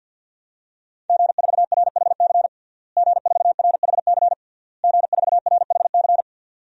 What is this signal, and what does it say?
Morse code, heard as a keyed tone, about 52 wpm, 700 Hz. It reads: G4RHC G4RHC G4RHC
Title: DXCC-100 Call Signs - 5 Characters - Repeated 3x - 50wpm